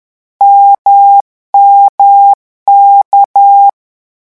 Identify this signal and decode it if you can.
Morse code, heard as a keyed tone, about 10 wpm, 800 Hz. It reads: MMK
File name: morse mp3 / lecon1-vitesse 10